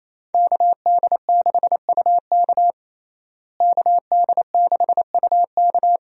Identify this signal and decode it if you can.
Morse code, heard as a keyed tone, about 27 wpm, 700 Hz. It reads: KD6UK KD6UK
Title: USA Call Signs - 5 Characters - Repeated 2x - 28wpm